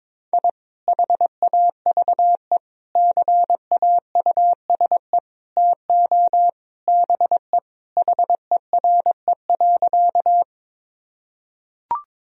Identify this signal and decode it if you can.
Morse code, heard as a keyed tone, about 22 wpm, 700 Hz. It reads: I HAVE CAUSE TO BE HERE.
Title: Sentences from Top 1000 Words - Review Part 1 - 22wpm